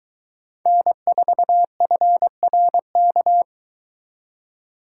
Morse code, heard as a keyed tone, about 23 wpm, 700 Hz. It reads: N4FRK